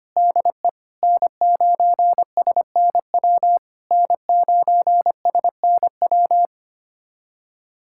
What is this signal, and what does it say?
Morse code, heard as a keyed tone, about 25 wpm, 700 Hz. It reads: DE N9SNW N9SNW
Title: SOTA QSO Practice 25wpm